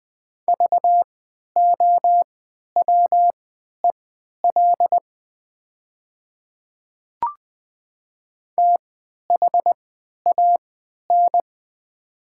Morse code, heard as a keyed tone, about 20 wpm, spaced as for 11 wpm, 700 Hz. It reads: VOWEL THAN
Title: 3x Character Spacing - Top 500 Words - 20wpm